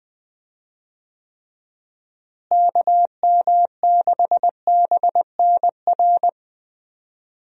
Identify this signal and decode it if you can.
Morse code, heard as a keyed tone, about 20 wpm, 700 Hz. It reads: KM6BNR
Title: USA Call Signs - 6 Characters - Repeated 3x - 20wpm